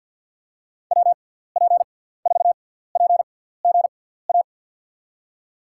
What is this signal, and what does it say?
Morse code, heard as a keyed tone, about 53 wpm, spaced as for 17 wpm, 700 Hz. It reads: WP4PGA
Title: USA Call Signs - 6 Characters - 6x Character Spacing - 50wpm